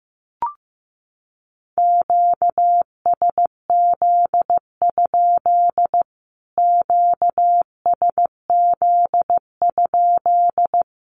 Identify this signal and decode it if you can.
Morse code, heard as a keyed tone, about 14 wpm, 700 Hz. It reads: QSZ? QSZ?
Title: Q-Codes Repeated 3x 15wpm